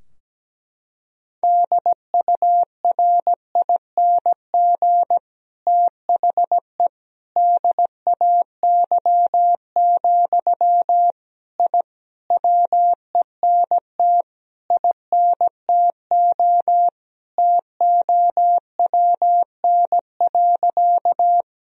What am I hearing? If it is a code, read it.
Morse code, heard as a keyed tone, about 17 wpm, 700 Hz. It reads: DURING THE DAY, I WENT INTO TOWN.
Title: Sentences from Top 1000 Words - Review Part 2 - 17wpm